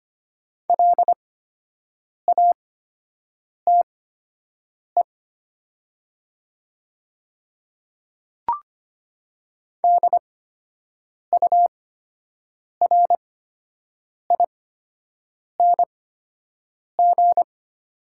Morse code, heard as a keyed tone, about 25 wpm, spaced as for 7 wpm, 700 Hz. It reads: LATE DURING
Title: 8x Character Spacing - Top 500 Words - 25wpm